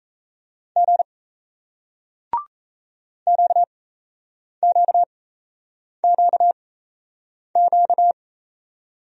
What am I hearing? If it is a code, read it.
Morse code, heard as a keyed tone, about 34 wpm, spaced as for 8 wpm, 700 Hz. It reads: G QQQQ
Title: Speed-Racing Single Letters 28wpm